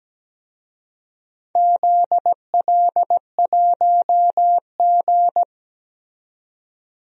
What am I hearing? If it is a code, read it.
Morse code, heard as a keyed tone, about 17 wpm, 700 Hz. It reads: ZL1G
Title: DXCC-100 Call Signs - 4 Characters - 17wpm